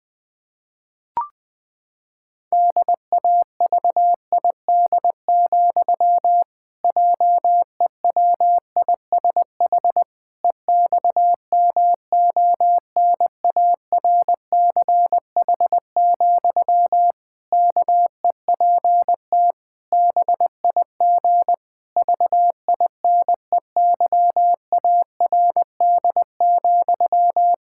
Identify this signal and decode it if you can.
Morse code, heard as a keyed tone, about 21 wpm, 700 Hz. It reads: DAVID, JEWISH EXMONARCH, KEPT BIG VINEYARD,
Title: Pangram Challenge v2 - 20wpm